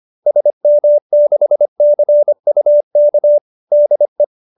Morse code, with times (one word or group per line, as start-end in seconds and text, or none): 0.3-3.4 SM6CUK
3.7-4.2 DE